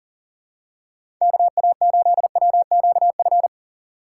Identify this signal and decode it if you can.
Morse code, heard as a keyed tone, about 38 wpm, 700 Hz. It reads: KA8WQF
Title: USA Call Signs - 6 Characters - 2x Character Spacing - 40wpm